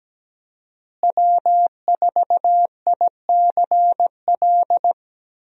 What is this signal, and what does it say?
Morse code, heard as a keyed tone, about 17 wpm, 700 Hz. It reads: W4ICL
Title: USA Call Signs - 5 Characters - Repeated 3x - 17wpm